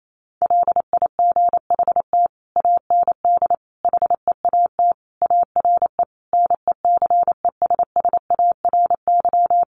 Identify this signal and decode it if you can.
Morse code, heard as a keyed tone, about 28 wpm, 700 Hz. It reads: LIGHT AND HEAT ARE NECESSARY
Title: Sentences from Top 1300 Words 28wpm